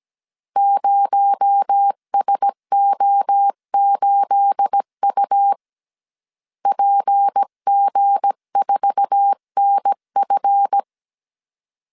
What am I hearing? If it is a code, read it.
Morse code, heard as a keyed tone, about 17 wpm, 800 Hz. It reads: 0SO8U PG4NF
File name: CW / DET / 1 9 1500C S B 16wpm 2856